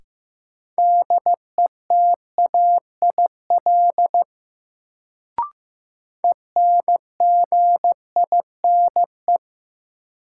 Morse code, heard as a keyed tone, about 15 wpm, 700 Hz. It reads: DETAIL ENGINE
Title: Sentences from Top 1500 Words - New Word Drill - 15wpm